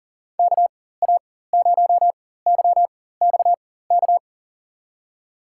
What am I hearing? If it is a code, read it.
Morse code, heard as a keyed tone, about 40 wpm, spaced as for 19 wpm, 700 Hz. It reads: KA0YXK
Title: USA Call Signs - 6 Characters - 4x Character Spacing - 40wpm